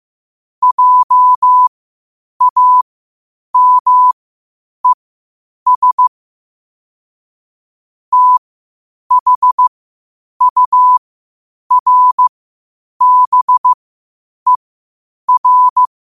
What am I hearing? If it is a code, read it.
Morse code, heard as a keyed tone, about 15 wpm, spaced as for 8 wpm, 1000 Hz. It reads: JAMES THURBER
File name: qotdfiles / 8wpm